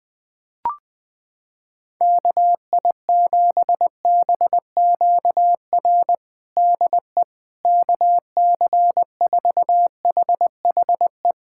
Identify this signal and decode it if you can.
Morse code, heard as a keyed tone, about 20 wpm, 700 Hz. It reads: KI7BQR DE KC4HHE
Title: USA Callsign DE Callsign 20wpm